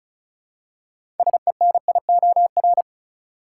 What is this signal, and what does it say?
Morse code, heard as a keyed tone, about 36 wpm, 700 Hz. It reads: SENIOR
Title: Six Letter Words - 35wpm